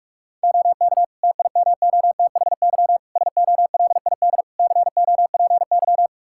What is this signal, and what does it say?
Morse code, heard as a keyed tone, about 46 wpm, 700 Hz. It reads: OK TIMOTHY SOLID COPY
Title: K1USN SST QSO Practice 45wpm